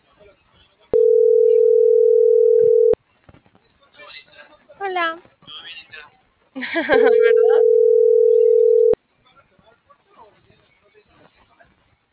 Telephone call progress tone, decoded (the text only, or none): ringback tone